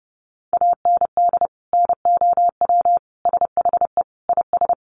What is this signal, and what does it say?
Morse code, heard as a keyed tone, about 28 wpm, 700 Hz. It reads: AND NOW SHE IS
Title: Sentences from Top 1500 Words 30wpm